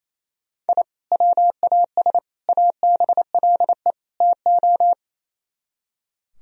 Morse code, heard as a keyed tone, about 36 wpm, 700 Hz. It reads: I WAS ABLE TO